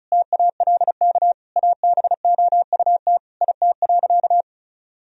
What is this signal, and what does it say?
Morse code, heard as a keyed tone, about 36 wpm, 700 Hz. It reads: TALK ABOUT IT.
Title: Sentences from Top 1000 Words - Review Part 4 - 35wpm